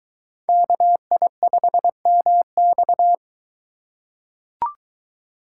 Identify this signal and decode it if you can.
Morse code, heard as a keyed tone, about 22 wpm, 700 Hz. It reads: KI5MX